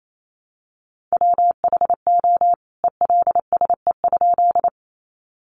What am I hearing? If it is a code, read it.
Morse code, heard as a keyed tone, about 28 wpm, 700 Hz. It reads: WHO ELSE?
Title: A Sound Of Thunder 28wpm